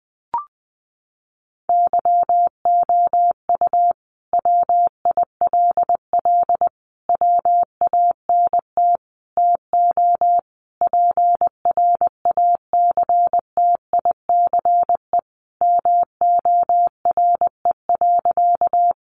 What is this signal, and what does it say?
Morse code, heard as a keyed tone, about 20 wpm, 700 Hz. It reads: YOU WILL WANT TO PRACTICE MORE.
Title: Sentences from Top 700 Words 20wpm